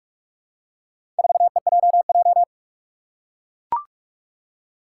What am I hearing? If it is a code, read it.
Morse code, heard as a keyed tone, about 41 wpm, 700 Hz. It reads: 4EJJ
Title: Four Challenging Letter-Number Combinations 45wpm